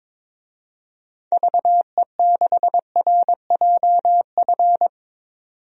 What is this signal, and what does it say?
Morse code, heard as a keyed tone, about 22 wpm, 700 Hz. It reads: VE6RJF
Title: DXCC-100 Call Signs - 6 Characters - 22wpm